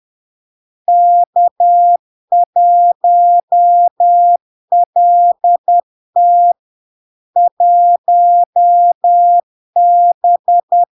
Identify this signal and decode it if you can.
Morse code, heard as a keyed tone, about 10 wpm, 700 Hz. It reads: K1LT 1B
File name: LARK-FD-2022-Log-in-CW-10wpm-700-Hz